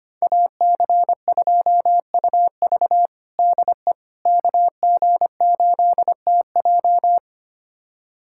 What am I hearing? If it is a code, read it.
Morse code, heard as a keyed tone, about 25 wpm, 700 Hz. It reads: AC2UV DE KG8TJ
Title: USA Callsign DE Callsign - 5 Characters - 25wpm